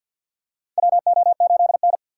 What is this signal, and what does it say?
Morse code, heard as a keyed tone, about 52 wpm, 700 Hz. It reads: WO8N
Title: USA Call Signs - 6x Character Spacing - 50wpm